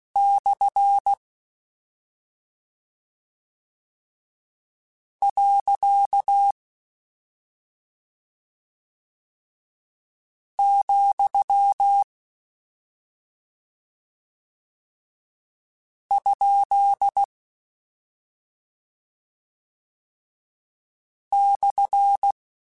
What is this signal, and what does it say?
Morse code, heard as a keyed tone, about 16 wpm, 800 Hz. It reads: / . , ? /
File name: Misc / Misc-mp3s / Punct @5